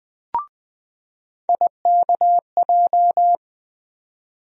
Morse code, heard as a keyed tone, about 20 wpm, 700 Hz. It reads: IKJ
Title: USA Call Sign Postfixes - 3 Letters 20wpm